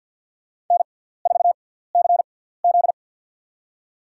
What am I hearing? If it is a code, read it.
Morse code, heard as a keyed tone, about 51 wpm, spaced as for 17 wpm, 700 Hz. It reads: N4CZ
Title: USA Call Signs - 4 Characters - 6x Character Spacing - 50wpm